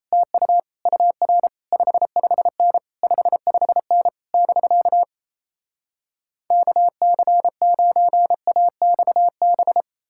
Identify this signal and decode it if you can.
Morse code, heard as a keyed tone, about 37 wpm, 700 Hz. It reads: TU UR 55N 55N <BK> KC9AXB